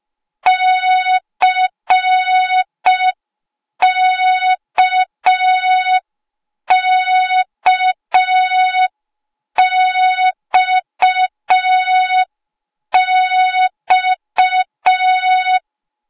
Morse code, heard as a keyed tone, about 5 wpm, 3700 Hz. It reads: CKKXX
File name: Morze-HA2MN / hang / 08KXC HA2MN